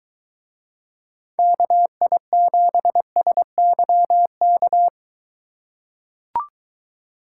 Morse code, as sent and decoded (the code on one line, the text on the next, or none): -.- .. --... ... -.-- -.-
KI7SYK